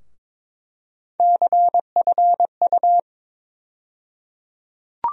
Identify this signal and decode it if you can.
Morse code, heard as a keyed tone, about 22 wpm, 700 Hz. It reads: CFU
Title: USA Call Sign Postfixes - 3 Letters 22wpm